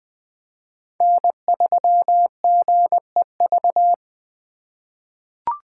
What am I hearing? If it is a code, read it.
Morse code, heard as a keyed tone, about 20 wpm, 700 Hz. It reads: N3GEV
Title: Code Groups - 5 Letter:Numbers - 20wpm